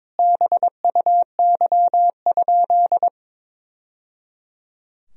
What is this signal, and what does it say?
Morse code, heard as a keyed tone, about 22 wpm, 700 Hz. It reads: BUY?
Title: Sentences from Top 1000 Words - Review Part 4 - 22wpm